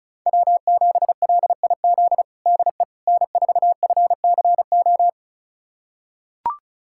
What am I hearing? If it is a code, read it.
Morse code, heard as a keyed tone, about 34 wpm, 690 Hz. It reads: W7LIZ DE N4FCO
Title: USA Callsign DE Callsign - 5 Characters - 35wpm